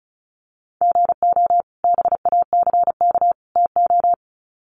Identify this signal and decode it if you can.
Morse code, heard as a keyed tone, about 34 wpm, 700 Hz. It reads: GO BACK TO
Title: Sets of 3 Words 35wpm